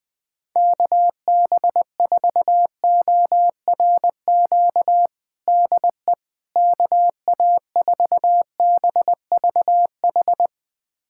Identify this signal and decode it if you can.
Morse code, heard as a keyed tone, about 21 wpm, 700 Hz. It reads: KB4ORQ DE KA4BVH
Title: USA Callsign DE Callsign - 6 Characters - 20wpm